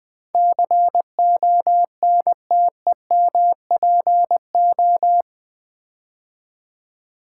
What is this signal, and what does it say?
Morse code, heard as a keyed tone, about 20 wpm, 700 Hz. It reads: CONTEMPO
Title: Build A Word - 4 Letters At A Time - 12-Letter Words - 20wpm